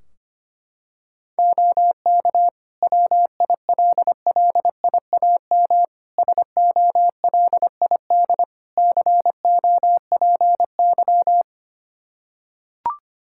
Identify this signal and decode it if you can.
Morse code, heard as a keyed tone, about 25 wpm, 700 Hz. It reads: OK WILLIAM SOLID COPY
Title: Practice QSO - Exchange 3 - More Info - 25wpm